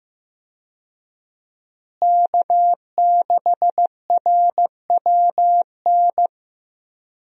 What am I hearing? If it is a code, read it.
Morse code, heard as a keyed tone, about 15 wpm, 700 Hz. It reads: K6RWN
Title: USA Call Signs - 5 Characters - Repeated 2x - 15wpm